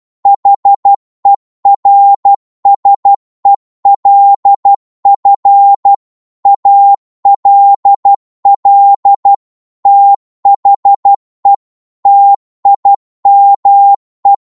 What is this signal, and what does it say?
Morse code, heard as a keyed tone, about 12 wpm, 800 Hz. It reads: HERSELF ALL THE TIME